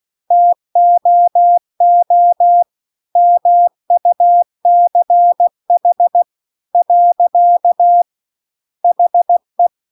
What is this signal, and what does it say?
Morse code, heard as a keyed tone, about 16 wpm, 700 Hz. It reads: TOO MUCH . HE
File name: CW-The-Big-Twoheaded-River-2-16WPM